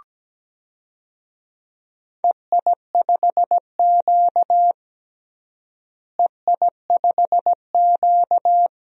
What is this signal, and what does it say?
Morse code, heard as a keyed tone, about 17 wpm, 700 Hz. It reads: EI5Q EI5Q
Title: DXCC-100 Call Signs - 4 Characters - Repeated 2x - 17wpm